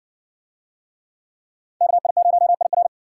Morse code, heard as a keyed tone, about 60 wpm, 700 Hz. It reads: KI0IR